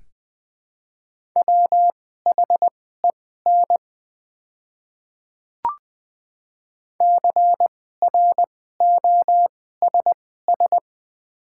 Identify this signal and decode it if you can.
Morse code, heard as a keyed tone, about 20 wpm, spaced as for 15 wpm, 700 Hz. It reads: WHEN CROSS
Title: 2x Character Spacing - Top 500 Words - 20wpm